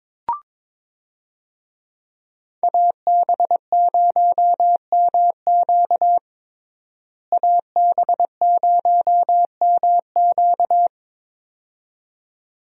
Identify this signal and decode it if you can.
Morse code, heard as a keyed tone, about 21 wpm, 700 Hz. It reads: AB0MQ AB0MQ
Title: USA Call Signs - 5 Characters - Repeated 2x - 22wpm